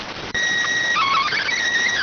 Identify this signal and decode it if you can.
SSTV (slow-scan television), PD 290.